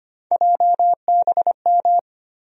Morse code, heard as a keyed tone, about 25 wpm, 700 Hz. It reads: JBM
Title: USA Call Sign Postfixes - 3 Letters 25wpm